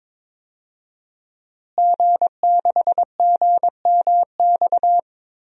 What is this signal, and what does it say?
Morse code, heard as a keyed tone, about 22 wpm, 700 Hz. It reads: G6GMX
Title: DXCC-100 Call Signs - 5 Characters - Repeated 3x - 22wpm